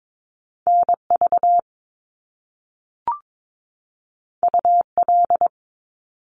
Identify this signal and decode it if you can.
Morse code, heard as a keyed tone, about 22 wpm, 700 Hz. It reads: NV UL
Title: Top Two Letter Combinations in Words 22wpm